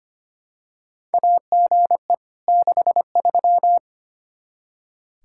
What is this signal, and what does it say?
Morse code, heard as a keyed tone, about 25 wpm, 700 Hz. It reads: AGE 63
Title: K1USN SST QSO Practice 25wpm